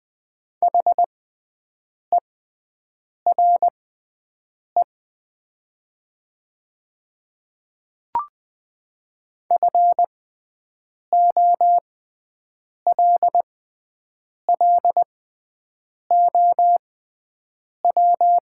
Morse code, heard as a keyed tone, about 20 wpm, spaced as for 7 wpm, 700 Hz. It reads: HERE FOLLOW
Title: 6x Character Spacing - Top 200 Words - 20wpm